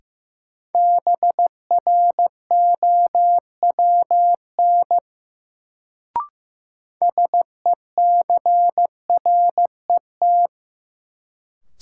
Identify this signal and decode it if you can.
Morse code, heard as a keyed tone, about 15 wpm, 700 Hz. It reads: BROWN SECRET
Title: Sentences from Top 1200 Words - New Word Drill - 15wpm